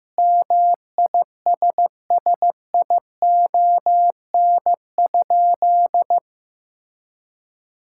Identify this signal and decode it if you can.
Morse code, heard as a keyed tone, about 15 wpm, 700 Hz. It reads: MISSION?
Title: Sentences from Top 1600 Words 15wpm